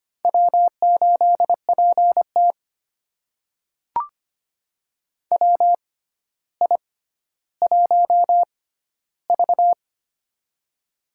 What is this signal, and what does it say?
Morse code, heard as a keyed tone, about 25 wpm, spaced as for 13 wpm, 700 Hz. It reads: W8PT WI1V
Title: USA Call Signs - 4 Characters - 6x Character Spacing - 25wpm